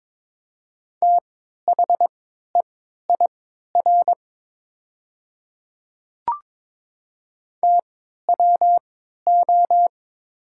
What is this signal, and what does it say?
Morse code, heard as a keyed tone, about 22 wpm, spaced as for 13 wpm, 700 Hz. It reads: THEIR TWO
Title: 3x Character Spacing - Top 100 Words - 22wpm